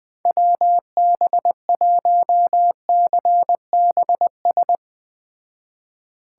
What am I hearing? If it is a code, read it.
Morse code, heard as a keyed tone, about 20 wpm, 700 Hz. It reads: WB1CBS
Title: Speed Racing - USA Call Signs 17wpm